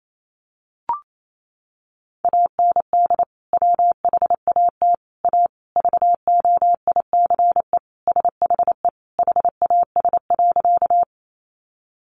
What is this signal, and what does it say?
Morse code, heard as a keyed tone, about 28 wpm, 700 Hz. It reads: AND WHAT A VOICE SHE HAS.
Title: Sentences from Top 500 Words 28wpm